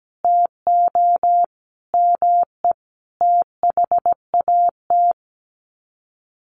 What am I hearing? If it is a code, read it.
Morse code, heard as a keyed tone, about 17 wpm, 700 Hz. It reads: TO ME THAT
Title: Sets of 5 Words Encore 17wpm